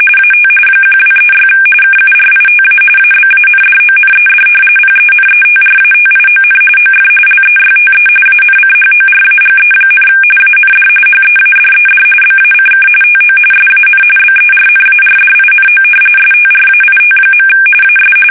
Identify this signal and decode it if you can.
RTTY, about 45.45 baud. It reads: AT 1236 KURITA ABANDONED PENETRATION OF LEYTE ANCHORAGE, AND HEADED BACK TO SAN BERNARDINO STRAIT. AS
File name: fsk850 / battleshipnj